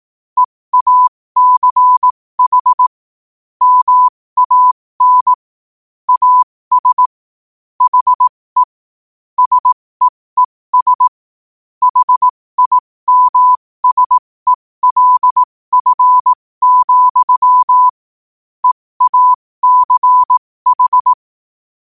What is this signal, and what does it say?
Morse code, heard as a keyed tone, about 18 wpm, spaced as for 15 wpm, 1000 Hz. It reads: EACH MAN AS HE SEES HIMSELF, EACH
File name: qotdfiles / 15wpm